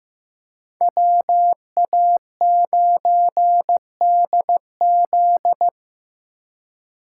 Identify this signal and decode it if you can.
Morse code, heard as a keyed tone, about 18 wpm, 700 Hz. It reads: WA9DZ
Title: Speed Racing - USA 2x2 Call Signs 15wpm